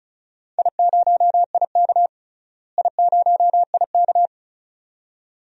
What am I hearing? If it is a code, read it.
Morse code, heard as a keyed tone, about 36 wpm, 700 Hz. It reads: I0IK I0IK
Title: DXCC-100 Call Signs - 4 Characters - Repeated 2x - 35wpm